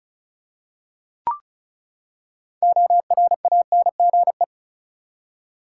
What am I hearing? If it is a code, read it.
Morse code, heard as a keyed tone, about 34 wpm, 700 Hz. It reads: ORANGE
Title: Six Letter Words - 35wpm